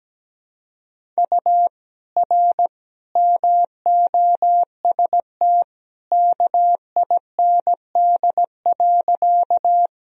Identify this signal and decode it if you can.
Morse code, heard as a keyed tone, about 17 wpm, 700 Hz. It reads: U R MOST KIND.
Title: Sentences from Top 200 Words and Abbreviations 17wpm